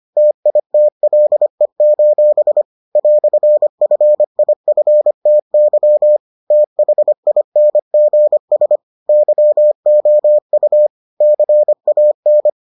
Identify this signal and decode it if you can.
Morse code, heard as a keyed tone, about 25 wpm, 600 Hz. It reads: TITLE: "FIFTY THINGS YOU CAN